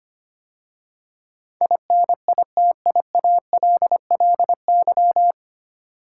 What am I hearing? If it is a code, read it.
Morse code, heard as a keyed tone, about 25 wpm, 700 Hz. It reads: INITIALLY
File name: practice / 25wpm / 9 Letter Words - First 3 Letters Given - 25wpm